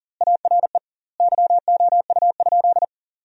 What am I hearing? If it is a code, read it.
Morse code, heard as a keyed tone, about 40 wpm, 700 Hz. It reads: ARE YOU?
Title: 2x Word-Spacing - Sentences from Top 200 Words - 40wpm